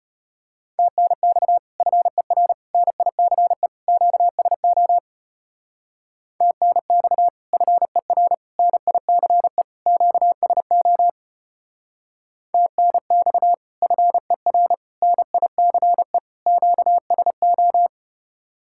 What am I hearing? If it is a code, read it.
Morse code, heard as a keyed tone, about 32 wpm, 700 Hz. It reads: TNX FER NICE QSO TNX FER NICE QSO TNX FER NICE QSO